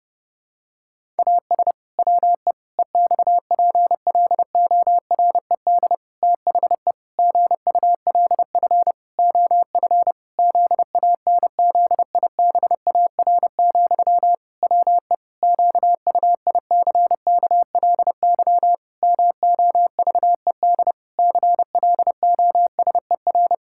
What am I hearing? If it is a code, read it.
Morse code, heard as a keyed tone, about 32 wpm, 700 Hz. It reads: AS WE EXPLORED THE GULF OF ZANZIBAR, WE QUICKLY MOVED CLOSER
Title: Pangram Challenge v2 - 30wpm